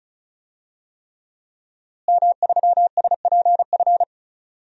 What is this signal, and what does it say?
Morse code, heard as a keyed tone, about 36 wpm, 700 Hz. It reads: M3SPF
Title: DXCC-100 Call Signs - 5 Characters - Repeated 3x - 35wpm